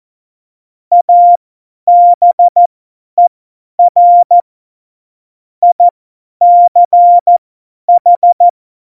Morse code, heard as a keyed tone, about 14 wpm, spaced as for 10 wpm, 700 Hz. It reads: ABER ICH
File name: Morsekurs / Witze / Witz 20. Oktober 2020, 50 F70